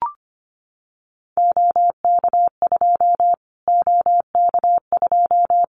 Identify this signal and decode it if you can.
Morse code, heard as a keyed tone, about 25 wpm, 700 Hz. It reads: OK2 OK2
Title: Callsign Prefixes for Top 25 DXCC - Repeated 3x - 25wpm